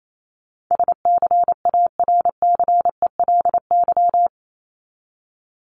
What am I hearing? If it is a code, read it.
Morse code, heard as a keyed tone, about 28 wpm, 700 Hz. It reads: SCARCELY
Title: LY Suffix - Top 100 Words with Base Words 28wpm